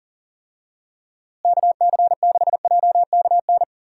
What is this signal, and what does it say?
Morse code, heard as a keyed tone, about 40 wpm, 700 Hz. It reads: KC6JKN
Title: USA Call Signs - 6 Characters - Repeated 2x - 40wpm